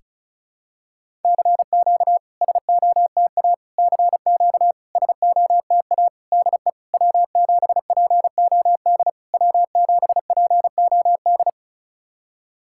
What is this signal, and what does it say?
Morse code, heard as a keyed tone, about 36 wpm, 690 Hz. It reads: CQ SOTA CQ SOTA DE W7POD W7POD
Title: SOTA QSO Practice 35wpm